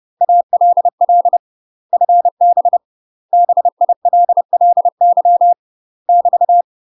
Morse code, heard as a keyed tone, about 30 wpm, 700 Hz. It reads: ALL FB BILLY <BT>
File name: UK MASTER QSOs TRAINING / 30WPM / UK MASTER QSOs 0048